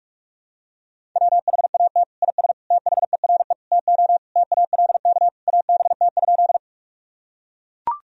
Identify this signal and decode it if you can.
Morse code, heard as a keyed tone, about 41 wpm, 700 Hz. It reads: WHAT IS THERE TO TALK ABT?
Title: Sentences from Top 400 Words and Abbreviations 45wpm